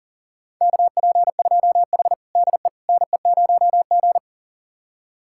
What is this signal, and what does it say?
Morse code, heard as a keyed tone, about 40 wpm, 700 Hz. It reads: KP2H DE NE0G
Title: USA Callsign DE Callsign - 4 Characters - 40wpm